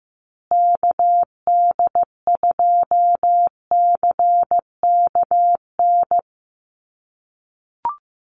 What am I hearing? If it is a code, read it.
Morse code, heard as a keyed tone, about 15 wpm, 700 Hz. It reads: KD2CKN
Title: USA Call Signs Repeated 2x 15wpm